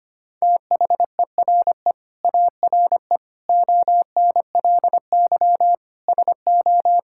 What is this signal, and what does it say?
Morse code, heard as a keyed tone, about 25 wpm, 700 Hz. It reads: THERE ARE ONLY SO
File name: Sets of 9 Words - 25wpm